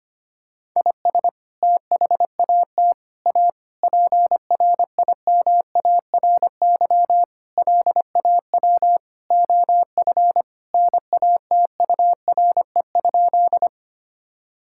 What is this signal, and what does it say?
Morse code, heard as a keyed tone, about 25 wpm, 700 Hz. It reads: IS THAT A PRIMARY LAW OF NATURE?
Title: Sentences from Top 1000 Words - Review Part 5 - 25wpm